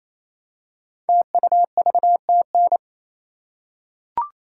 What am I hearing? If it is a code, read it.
Morse code, heard as a keyed tone, about 27 wpm, 700 Hz. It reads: TUVTN
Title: Code Groups - 5 Letters - 28wpm